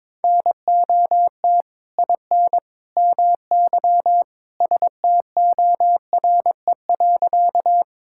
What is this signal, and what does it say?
Morse code, heard as a keyed tone, about 22 wpm, 700 Hz. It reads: NOT IN MY STORE.
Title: Sentences from Top 1000 Words - Review Part 5 - 22wpm